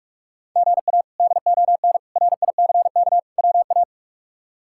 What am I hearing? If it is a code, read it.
Morse code, heard as a keyed tone, about 46 wpm, 700 Hz. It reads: GA DON RICK WA
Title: K1USN SST QSO Practice - 45wpm